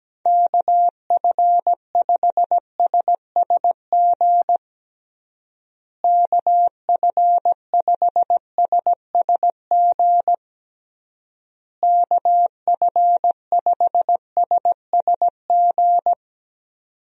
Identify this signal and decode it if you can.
Morse code, heard as a keyed tone, about 17 wpm, 700 Hz. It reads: KF5SSG KF5SSG KF5SSG
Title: USA Call Signs - 6 Characters - Repeated 3x - 17wpm